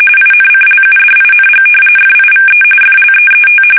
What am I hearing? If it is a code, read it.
RTTY, about 45.45 baud. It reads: INCLUDING FIVE FLEET
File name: fsk850 / battleshipnj